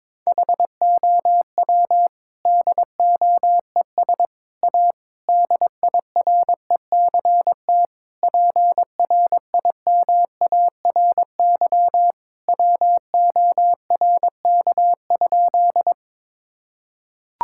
Morse code, heard as a keyed tone, about 22 wpm, 700 Hz. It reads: HOW DOES A DIRECT PRIMARY WORK?
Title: Sentences from Top 1000 Words - Review Part 5 - 22wpm